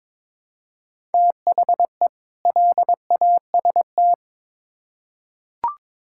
Morse code, heard as a keyed tone, about 22 wpm, 700 Hz. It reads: THE LAST
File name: Build A Phrase - 2-Word Phrases - 22wpm